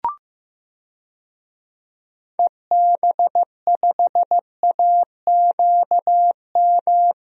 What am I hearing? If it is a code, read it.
Morse code, heard as a keyed tone, about 14 wpm, 700 Hz. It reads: EB5AQM
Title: DXCC-100 Call Signs - 6 Characters - Repeated 3x - 15wpm